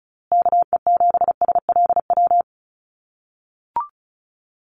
Morse code, heard as a keyed tone, about 34 wpm, 700 Hz. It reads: KE7SLW
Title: USA Call Signs Repeated 2x 35wpm